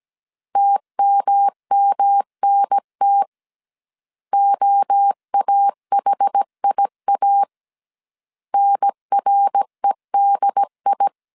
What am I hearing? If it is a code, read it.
Morse code, heard as a keyed tone, about 17 wpm, 800 Hz. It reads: TMMNT OAHIA NREDI